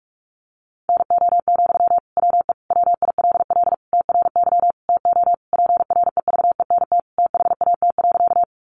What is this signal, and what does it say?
Morse code, heard as a keyed tone, about 44 wpm, 700 Hz. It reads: NO, WE WILL TRY TO PREVENT THAT.
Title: Sentences from Top 1100 Words 45wpm